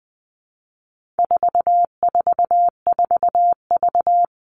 Morse code, heard as a keyed tone, about 20 wpm, 700 Hz. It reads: 444V
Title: All Cut Numbers - 4-Digit 20wpm